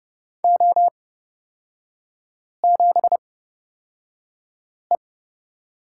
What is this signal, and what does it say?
Morse code, heard as a keyed tone, about 30 wpm, spaced as for 5 wpm, 700 Hz. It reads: O7E
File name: Lesson 37e - Letter x - Three Characters 30wpm